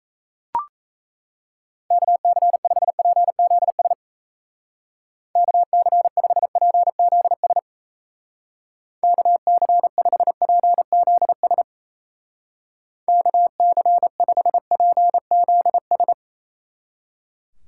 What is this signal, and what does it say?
Morse code, heard as a keyed tone, about 33 wpm, 700 Hz. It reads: KC5PZS KC5PZS KC5PZS KC5PZS
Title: Speed Racing - USA 2x3 Call Signs 28wpm